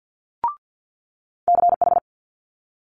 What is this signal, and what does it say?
Morse code, heard as a keyed tone, about 50 wpm, 700 Hz. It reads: CH
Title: Multi-letter Phonemes Quick Fire 50wpm